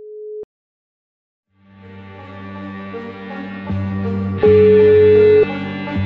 Ringback tone (Europe, 425 Hz).